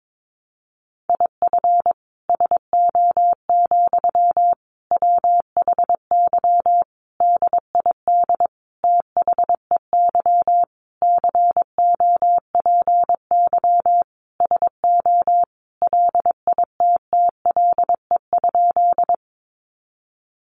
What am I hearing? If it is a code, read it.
Morse code, heard as a keyed tone, about 22 wpm, 700 Hz. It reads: IF SO, WHY DID THEY COPY SO LITTLE?
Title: Sentences from Top 1400 Words 22wpm